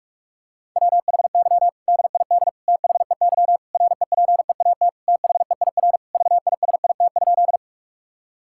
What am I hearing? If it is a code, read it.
Morse code, heard as a keyed tone, about 46 wpm, 700 Hz. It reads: WHY DID THEY REPEAT THEIR VISIT?
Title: Sentences from Top 1600 Words 45wpm